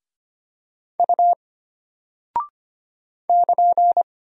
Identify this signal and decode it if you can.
Morse code, heard as a keyed tone, about 25 wpm, 700 Hz. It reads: U <KN>
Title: Single Character (Letter:Number:Punctuation:Prosign) Alt 25wpm